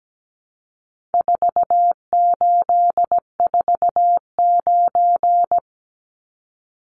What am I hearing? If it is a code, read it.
Morse code, heard as a keyed tone, about 17 wpm, 700 Hz. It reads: 4849
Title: Numbers - 4-Digit 17wpm